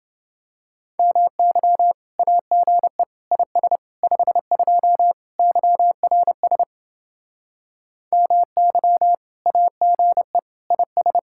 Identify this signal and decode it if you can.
Morse code, heard as a keyed tone, about 29 wpm, 700 Hz. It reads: MY AGE IS 52 YRS MY AGE IS